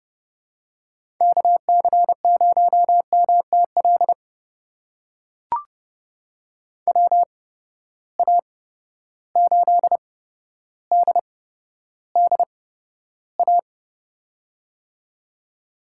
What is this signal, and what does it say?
Morse code, heard as a keyed tone, about 31 wpm, spaced as for 13 wpm, 700 Hz. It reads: KC0MTL WA8DDA
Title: USA Call Signs - 6 Characters - 8x Character Spacing - 30wpm